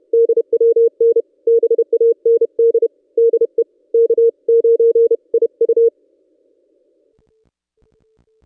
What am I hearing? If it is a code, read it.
Morse code, heard as a keyed tone, about 31 wpm, 460 Hz. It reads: DWN BAND DE K9IU
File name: contests / ss-cw-k5tr-op-n5rz -1600